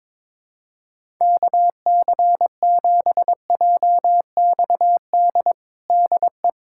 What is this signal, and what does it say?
Morse code, heard as a keyed tone, about 22 wpm, 700 Hz. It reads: KC7JXD DE
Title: USA Callsign DE Callsign - 6 Characters - 22wpm